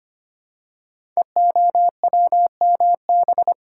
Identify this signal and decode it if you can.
Morse code, heard as a keyed tone, about 25 wpm, 700 Hz. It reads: EOWMB
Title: Code Groups - 5 Letter:Numbers - 25wpm